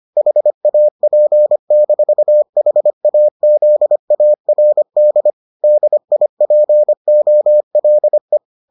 Morse code, heard as a keyed tone, about 25 wpm, 600 Hz. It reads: HAP-HAZARD DIPOLE